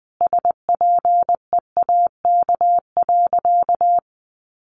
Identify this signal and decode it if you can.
Morse code, heard as a keyed tone, about 20 wpm, 700 Hz. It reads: SPEAK.
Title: Sentences from Top 1100 Words 20wpm